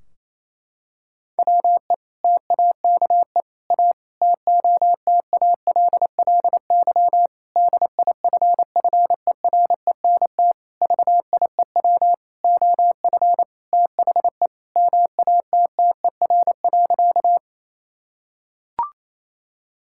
Morse code, heard as a keyed tone, about 28 wpm, 700 Hz. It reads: WE TAKE A TOTALLY DIFFERENT VIEW OF THE MATTER.
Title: Sentences from Top 1600 Words 28wpm